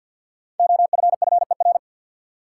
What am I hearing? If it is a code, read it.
Morse code, heard as a keyed tone, about 52 wpm, 700 Hz. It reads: OFFER
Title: Five Letter Words - 50wpm